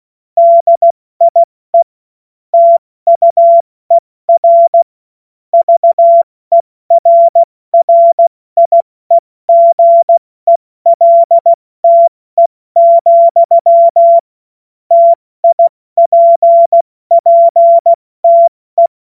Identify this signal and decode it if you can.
Morse code, heard as a keyed tone, about 16 wpm, 680 Hz. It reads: DIE TUER VERRIEGELTE, TIPPTE
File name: Morsekurs / Witze / Witz 16 Januar 2024 70 F80